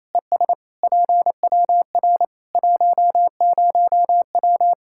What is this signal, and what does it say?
Morse code, heard as a keyed tone, about 28 wpm, 700 Hz. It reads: ES PWR 10W
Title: Practice QSO - Exchange 2 - Info - 28wpm